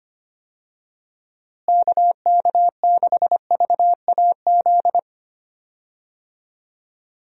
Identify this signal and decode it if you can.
Morse code, heard as a keyed tone, about 25 wpm, 700 Hz. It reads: KK6VAZ
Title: USA Call Signs - 6 Characters - Repeated 3x - 25wpm